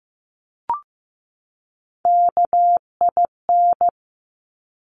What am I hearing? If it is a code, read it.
Morse code, heard as a keyed tone, about 15 wpm, 700 Hz. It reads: KIN
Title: Top Three Letter Combinations in Words 15wpm